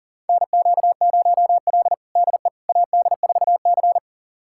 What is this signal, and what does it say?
Morse code, heard as a keyed tone, about 37 wpm, 700 Hz. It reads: NQ0L DE AD4C
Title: USA Callsign DE Callsign - 4 Characters - 40wpm